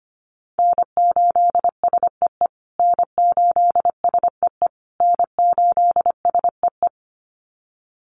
Morse code, heard as a keyed tone, about 25 wpm, 700 Hz. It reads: N8SEE N8SEE N8SEE
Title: USA Call Signs Repeated 3x 25wpm